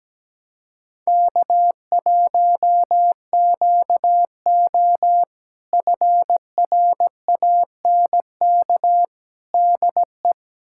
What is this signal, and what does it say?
Morse code, heard as a keyed tone, about 17 wpm, 700 Hz. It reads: K1QO FRANK DE
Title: K1USN SST QSO Practice - 17wpm